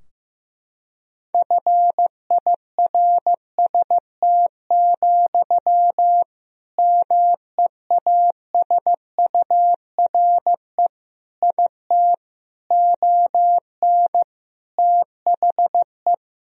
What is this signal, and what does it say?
Morse code, heard as a keyed tone, about 15 wpm, 700 Hz. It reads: FIRST, MEASURE IT ON THE
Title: Sentences from Top 1000 Words - Review Part 3 - 15wpm